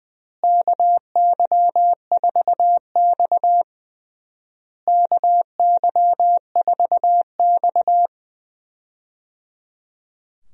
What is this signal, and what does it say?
Morse code, heard as a keyed tone, about 21 wpm, 700 Hz. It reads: KY4X KY4X
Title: USA Call Signs - 4 Characters - Repeated 3x - 20wpm